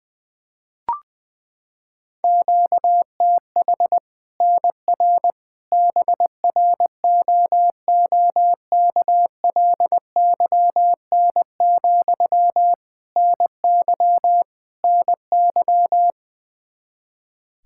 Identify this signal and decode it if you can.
Morse code, heard as a keyed tone, about 20 wpm, 700 Hz. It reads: QTH NR BROOKLYN, NY NY
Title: Practice QSO - Exchange 1 - Beginning - 20wpm